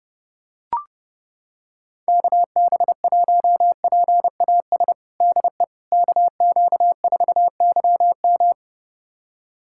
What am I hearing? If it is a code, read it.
Morse code, heard as a keyed tone, about 30 wpm, 700 Hz. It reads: KB1PAS DE KQ4YM
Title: USA Callsign DE Callsign 30wpm